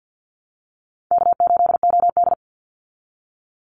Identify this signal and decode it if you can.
Morse code, heard as a keyed tone, about 50 wpm, 700 Hz. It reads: K8OD
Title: Call Signs Repeated 3x - 50wpm